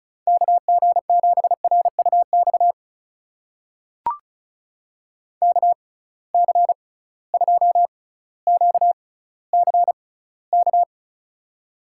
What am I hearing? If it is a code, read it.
Morse code, heard as a keyed tone, about 36 wpm, spaced as for 18 wpm, 700 Hz. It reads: KG7RUX KC2QCK
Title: USA Call Signs - 6 Characters - 6x Character Spacing - 35wpm